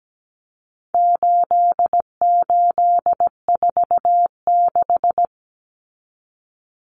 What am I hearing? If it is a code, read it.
Morse code, heard as a keyed tone, about 17 wpm, 700 Hz. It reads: 8846
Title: Numbers - 4-Digit 17wpm